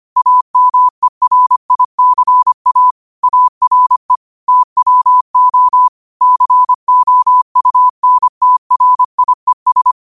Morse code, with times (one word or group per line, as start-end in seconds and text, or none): 0.2-2.9 AMERICA
3.2-4.2 ARE
4.5-5.9 TWO
6.2-9.9 COUNTRIES